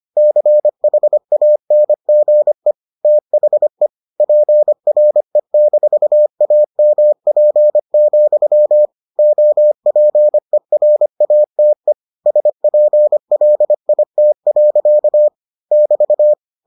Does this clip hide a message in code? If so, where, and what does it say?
Yes: Morse code from 0.2 to 16.3 s, heard as a keyed tone, about 25 wpm, 600 Hz. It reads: CHANGE THE PRE-AMP, OPERATE SPLIT. <BT>